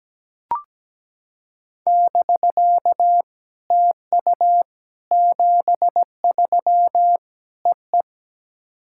Morse code, heard as a keyed tone, about 17 wpm, 700 Hz. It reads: <BK> TU 73 EE
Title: SOTA QSO Practice 17wpm